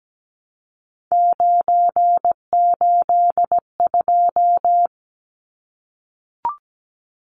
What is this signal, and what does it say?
Morse code, heard as a keyed tone, about 17 wpm, 700 Hz. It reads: 982
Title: Numbers - 3-Digit 17wpm